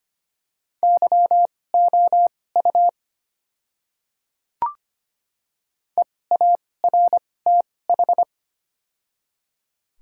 Morse code, heard as a keyed tone, about 25 wpm, spaced as for 18 wpm, 690 Hz. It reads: YOU EARTH
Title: 2x Character Spacing - Top 200 Words - 25wpm